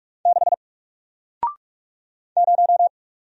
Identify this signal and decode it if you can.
Morse code, heard as a keyed tone, about 44 wpm, 700 Hz. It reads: 6 0
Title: Single Letter-Number Alt 45wpm